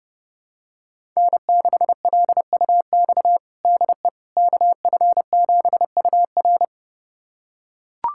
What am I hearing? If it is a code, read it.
Morse code, heard as a keyed tone, about 31 wpm, 700 Hz. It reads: N6LUX DE KF7UR
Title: USA Callsign DE Callsign - 5 Characters - 30wpm